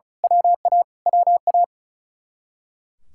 Morse code, heard as a keyed tone, about 33 wpm, 700 Hz. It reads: WA WA
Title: Speed-Racing Common QSO Phrases - 35wpm